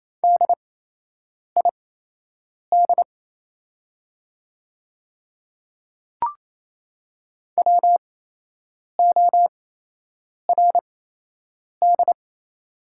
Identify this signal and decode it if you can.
Morse code, heard as a keyed tone, about 28 wpm, spaced as for 8 wpm, 700 Hz. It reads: DID WORD
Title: 8x Character Spacing - Top 200 Words - 28wpm